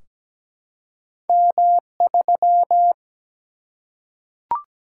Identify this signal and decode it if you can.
Morse code, heard as a keyed tone, about 17 wpm, 700 Hz. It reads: M3
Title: Single-Letter Single-Number 17wpm